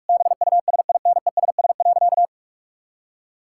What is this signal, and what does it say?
Morse code, heard as a keyed tone, about 46 wpm, 700 Hz. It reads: BUSINESS.
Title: Sentences from Top 1000 Words - Review Part 6 - 45wpm